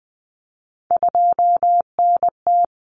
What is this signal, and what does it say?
Morse code, heard as a keyed tone, about 20 wpm, 700 Hz. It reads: 2NT
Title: Common Cut Numbers - 3-Digit 20wpm